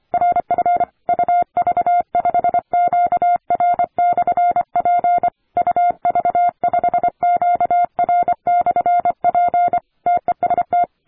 Morse code, heard as a keyed tone, about 25 wpm, 700 Hz. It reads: RF UV5QR/P UV5QR/P TEST
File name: attachments / 40m UU7JF P